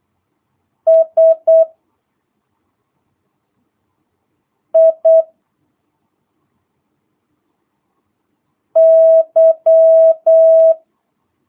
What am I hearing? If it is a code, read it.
Morse code, heard as a keyed tone, about 8 wpm, 650 Hz. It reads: S I Y